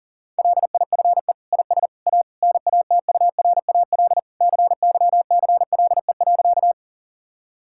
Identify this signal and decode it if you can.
Morse code, heard as a keyed tone, about 37 wpm, 700 Hz. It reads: LIFE IS A NATURAL CYCLE.
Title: Sentences from Top 1000 Words - Review Part 6 - 40wpm